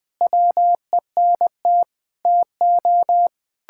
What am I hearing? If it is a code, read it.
Morse code, heard as a keyed tone, about 21 wpm, 700 Hz. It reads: WENT TO
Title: Sentences from Top 1600 Words 20wpm